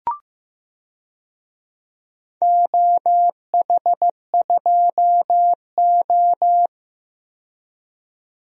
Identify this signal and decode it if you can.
Morse code, heard as a keyed tone, about 15 wpm, 700 Hz. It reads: OH2O
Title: DXCC-100 Call Signs - 4 Characters - 15wpm